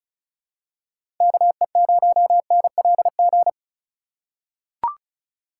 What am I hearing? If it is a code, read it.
Morse code, heard as a keyed tone, about 36 wpm, 700 Hz. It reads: KE0NLG
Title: USA Call Signs - 6 Characters - 2x Character Spacing - 35wpm